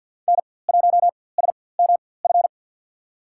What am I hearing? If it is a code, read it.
Morse code, heard as a keyed tone, about 52 wpm, spaced as for 23 wpm, 700 Hz. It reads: N1SMF
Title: USA Call Signs - 5 Characters - 4x Character Spacing - 50wpm